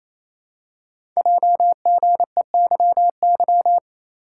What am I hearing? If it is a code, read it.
Morse code, heard as a keyed tone, about 28 wpm, 700 Hz. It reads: JGEYY
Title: Code Groups - 5 Letters - 28wpm